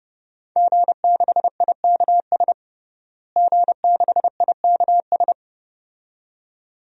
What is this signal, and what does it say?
Morse code, heard as a keyed tone, about 32 wpm, 700 Hz. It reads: G6IKS G6IKS
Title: DXCC-100 Call Signs - 5 Characters - Repeated 2x - 30wpm